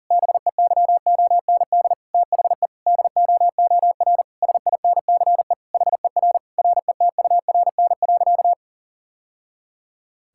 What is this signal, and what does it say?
Morse code, heard as a keyed tone, about 40 wpm, 700 Hz. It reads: BEYOND THE DOOR SINCE HER RETURN.
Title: Sentences from Top 1000 Words - Review Part 6 - 40wpm